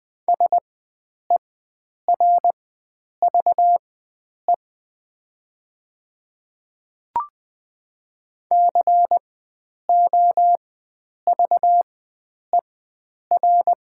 Morse code, heard as a keyed tone, about 20 wpm, spaced as for 9 wpm, 700 Hz. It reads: SERVE COVER
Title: 4x Character Spacing - Top 500 Words - 20wpm